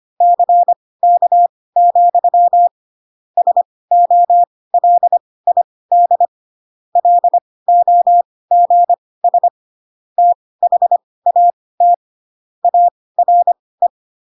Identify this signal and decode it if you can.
Morse code, heard as a keyed tone, about 25 wpm, spaced as for 18 wpm, 700 Hz. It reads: CK, SOLID LOGS THAT ARE
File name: Pinocchio / Pinocchio ch1 12 25 ik1hge